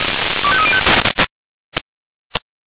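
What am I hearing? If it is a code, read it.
Five-tone selective call (ZVEI), heard as quick runs of five tones, tones of about 70 ms. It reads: 25305
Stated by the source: ZVEI five-tone selective call received on SDR.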